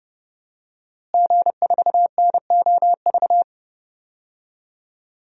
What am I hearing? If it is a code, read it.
Morse code, heard as a keyed tone, about 32 wpm, 700 Hz. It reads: G4NOV
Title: DXCC-100 Call Signs - 5 Characters - Repeated 2x - 30wpm